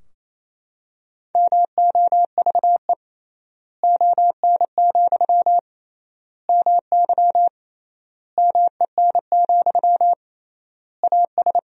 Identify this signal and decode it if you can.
Morse code, heard as a keyed tone, about 27 wpm, 700 Hz. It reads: MOVE ON, MY MEN, AS
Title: 3x Word-Spacing - Sentences from Top 500 Words - 28wpm